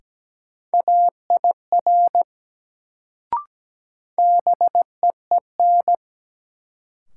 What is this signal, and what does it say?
Morse code, heard as a keyed tone, about 17 wpm, 700 Hz. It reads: AIR BEEN
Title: Sets of 2 Words - New Word Drill - 17wpm